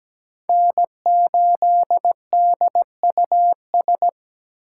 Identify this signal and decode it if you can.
Morse code, heard as a keyed tone, about 17 wpm, 700 Hz. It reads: N8DUS
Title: K1USN SST QSO Practice - 17wpm